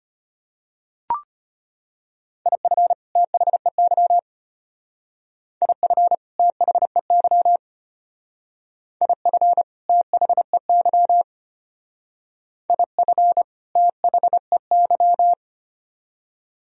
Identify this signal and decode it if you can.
Morse code, heard as a keyed tone, about 32 wpm, 700 Hz. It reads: IF THEY IF THEY IF THEY IF THEY